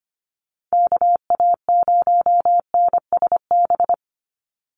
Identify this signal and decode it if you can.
Morse code, heard as a keyed tone, about 25 wpm, 700 Hz. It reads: KA0NSB
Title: USA Call Signs 25wpm